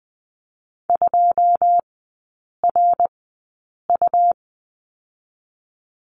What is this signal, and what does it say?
Morse code, heard as a keyed tone, about 20 wpm, 700 Hz. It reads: 2 R U
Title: 121 Lesson 20e - Number 3 - Three Characters 20wpm